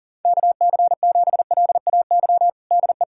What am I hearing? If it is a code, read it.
Morse code, heard as a keyed tone, about 37 wpm, 700 Hz. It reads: KC7LAY DE
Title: USA Callsign DE Callsign - 6 Characters - 40wpm